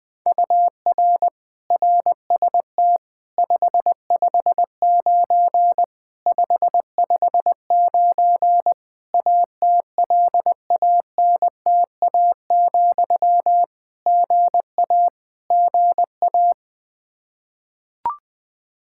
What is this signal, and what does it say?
Morse code, heard as a keyed tone, about 20 wpm, 700 Hz. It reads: UR RST 559 559 ATLANTA, GA GA
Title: K1USN SST QSO Practice 20wpm